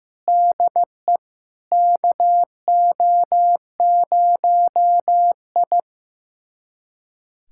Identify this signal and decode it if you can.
Morse code, heard as a keyed tone, about 15 wpm, 700 Hz. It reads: DE KO0I
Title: USA Callsign DE Callsign - 4 Characters - 15wpm